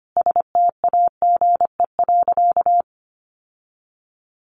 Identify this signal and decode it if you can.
Morse code, heard as a keyed tone, about 25 wpm, 700 Hz. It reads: STAGE.
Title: Sentences from Top 1200 Words 25wpm